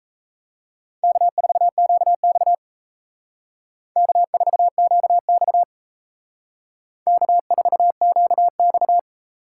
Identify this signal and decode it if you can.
Morse code, heard as a keyed tone, about 36 wpm, 700 Hz. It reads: K4QX K4QX K4QX